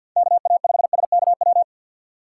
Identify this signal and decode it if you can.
Morse code, heard as a keyed tone, about 49 wpm, 700 Hz. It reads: KA5SKW
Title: USA Call Signs - 6 Characters - 3x Character Spacing - 50wpm